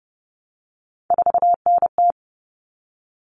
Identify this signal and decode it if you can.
Morse code, heard as a keyed tone, about 30 wpm, 700 Hz. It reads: 4NT
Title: Common Cut Numbers - 3-Digit 30wpm